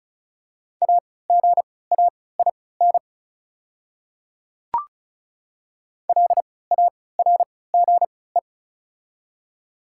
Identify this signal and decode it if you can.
Morse code, heard as a keyed tone, about 36 wpm, spaced as for 20 wpm, 700 Hz. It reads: AGAIN LARGE
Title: 3x Character Spacing - Top 200 Words - 35wpm